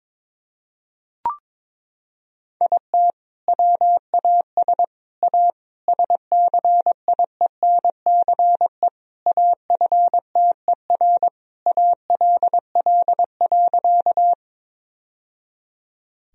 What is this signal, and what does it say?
Morse code, heard as a keyed tone, about 22 wpm, 700 Hz. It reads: IT WAS A SCIENCE AFTER ALL.
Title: Sentences from Top 1000 Words - Review Part 2 - 22wpm